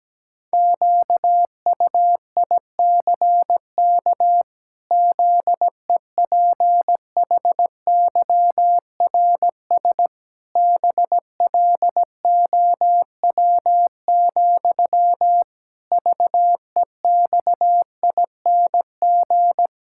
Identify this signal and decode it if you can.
Morse code, heard as a keyed tone, about 17 wpm, 700 Hz. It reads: QUICK ZEPHYRS BLOW, VEXING
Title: Pangram Challenge v2 - 17wpm